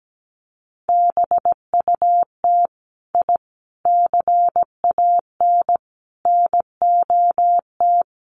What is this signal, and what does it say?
Morse code, heard as a keyed tone, about 17 wpm, 700 Hz. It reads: BUT I CAN NOT
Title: Sets of 4 Words 17wpm